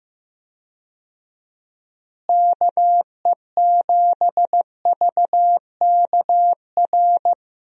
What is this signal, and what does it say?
Morse code, heard as a keyed tone, about 15 wpm, 700 Hz. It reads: KE7VKR
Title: USA Call Signs - 6 Characters - Repeated 3x - 15wpm